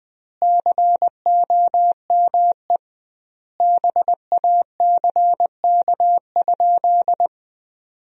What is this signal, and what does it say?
Morse code, heard as a keyed tone, about 20 wpm, 700 Hz. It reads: COME BACK?
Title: 2x Word-Spacing - Sentences from Top 300 Words - 20wpm